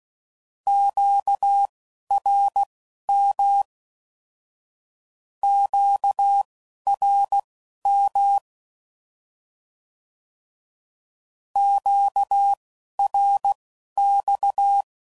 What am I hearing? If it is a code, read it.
Morse code, heard as a keyed tone, about 16 wpm, spaced as for 11 wpm, 800 Hz. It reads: QRM QRM QRX
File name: CW Code Groups @10WPM / Q Calls / Q Calls Twice